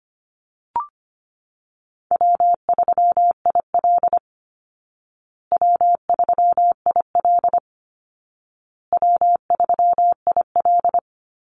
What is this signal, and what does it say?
Morse code, heard as a keyed tone, about 25 wpm, 700 Hz. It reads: W3IL W3IL W3IL
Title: Easy Call Signs Repeated 3x - 25wpm